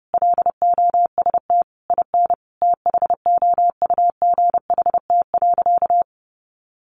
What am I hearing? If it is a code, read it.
Morse code, heard as a keyed tone, about 30 wpm, 700 Hz. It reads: LOST IN THOUGHT.
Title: Sentences from Top 1400 Words 30wpm